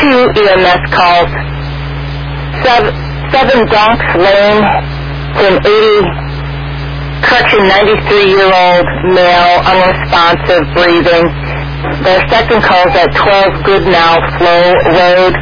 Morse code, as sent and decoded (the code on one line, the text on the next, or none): none